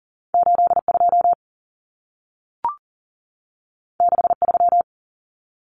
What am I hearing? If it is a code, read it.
Morse code, heard as a keyed tone, about 40 wpm, 700 Hz. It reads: 82 63
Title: Common Cut Numbers - 2-Digit 40wpm